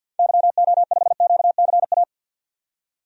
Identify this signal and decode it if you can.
Morse code, heard as a keyed tone, about 52 wpm, 700 Hz. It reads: YO5QCA
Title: DXCC-100 Call Signs - 6 Characters - 50wpm